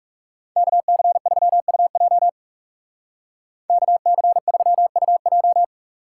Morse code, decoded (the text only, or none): KC3UJ KC3UJ